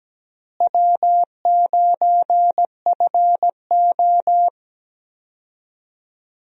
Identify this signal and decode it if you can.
Morse code, heard as a keyed tone, about 17 wpm, 700 Hz. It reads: W9FO
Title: USA Call Signs - 4 Characters - Repeated 3x - 17wpm